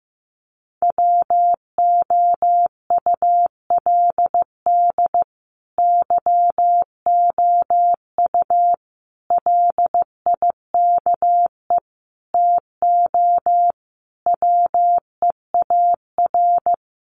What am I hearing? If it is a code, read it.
Morse code, heard as a keyed tone, about 14 wpm, 700 Hz. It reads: WOULD YOU LIKE TO WEAR
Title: Sentences from Top 1500 Words 15wpm